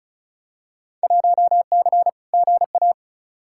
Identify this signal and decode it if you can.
Morse code, heard as a keyed tone, about 36 wpm, 700 Hz. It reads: 1C GA
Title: Field Day Exchanges - 35wpm